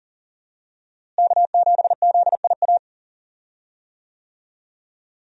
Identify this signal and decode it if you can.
Morse code, heard as a keyed tone, about 37 wpm, 700 Hz. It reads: K7ZIA
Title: USA Call Signs - 5 Characters - Repeated 2x - 40wpm